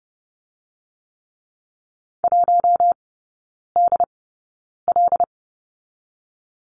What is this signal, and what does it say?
Morse code, heard as a keyed tone, about 30 wpm, spaced as for 9 wpm, 700 Hz. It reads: 1DL
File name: Lesson 17e - Letter u - Three Characters 30wpm